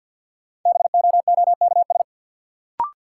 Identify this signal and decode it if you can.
Morse code, heard as a keyed tone, about 49 wpm, 700 Hz. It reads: BOOKS
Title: Sentences from Top 1000 Words - Review Part 4 - New Word Drill - 50wpm